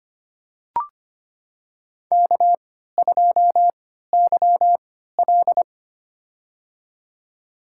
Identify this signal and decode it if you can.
Morse code, heard as a keyed tone, about 25 wpm, spaced as for 14 wpm, 700 Hz. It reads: K2YL
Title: USA Call Signs - 4 Characters - 3x Character Spacing - 25wpm